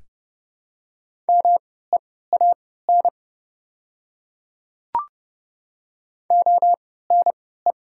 Morse code, heard as a keyed tone, about 31 wpm, spaced as for 17 wpm, 700 Hz. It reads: MEAN ONE
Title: 3x Character Spacing - Top 500 Words - 30wpm